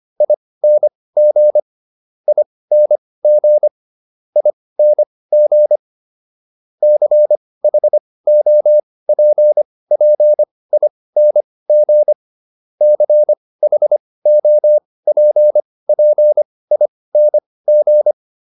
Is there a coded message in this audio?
Yes: Morse code from 0.2 to 18.1 s, heard as a keyed tone, about 25 wpm, spaced as for 18 wpm, 600 Hz. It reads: ING ING ING CHOPPING CHOPPING